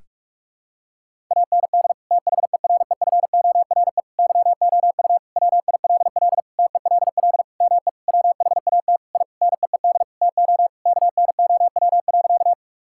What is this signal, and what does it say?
Morse code, heard as a keyed tone, about 46 wpm, 700 Hz. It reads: AND THEREFORE YOU WILL TELL ME WHAT I NEED TO KNOW.
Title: Sentences from Top 1000 Words - Review Part 5 - 45wpm